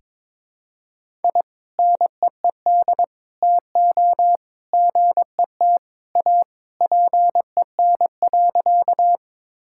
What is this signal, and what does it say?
Morse code, heard as a keyed tone, about 22 wpm, 700 Hz. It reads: I NEED TO GET A PEN.
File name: practice / 22wpm / Sentences with 2-Word Phrases - 22wpm